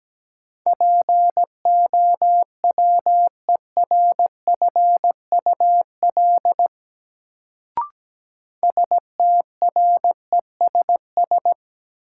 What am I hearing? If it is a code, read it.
Morse code, heard as a keyed tone, about 17 wpm, 700 Hz. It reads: POWERFUL STRESS
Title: Sentences from Top 1000 Words - Review Part 6 - New Word Drill - 17wpm